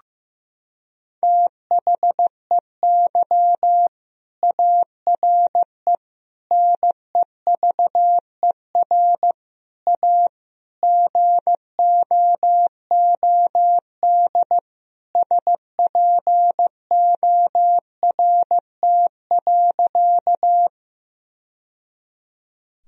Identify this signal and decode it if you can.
Morse code, heard as a keyed tone, about 16 wpm, 700 Hz. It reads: THEY ARE NEVER A GOOD SPORT.
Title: Sentences from Top 1000 Words - Review Part 4 - 15wpm